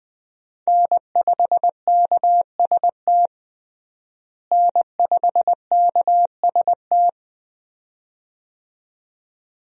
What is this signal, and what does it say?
Morse code, heard as a keyed tone, about 21 wpm, 700 Hz. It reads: N5KST N5KST
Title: USA Call Signs - 5 Characters - Repeated 2x - 20wpm